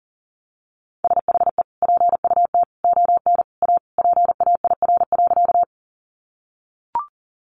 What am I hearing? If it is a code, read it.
Morse code, heard as a keyed tone, about 40 wpm, 700 Hz. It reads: SHE PUT ON A PAIR.
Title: Sentences from Top 1200 Words 40wpm